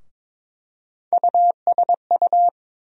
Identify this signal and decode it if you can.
Morse code, heard as a keyed tone, about 22 wpm, 700 Hz. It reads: USU
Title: USA Call Sign Postfixes - 3 Letters 22wpm